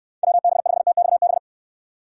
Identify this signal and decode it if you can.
Morse code, heard as a keyed tone, about 65 wpm, 700 Hz. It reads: WB5TXB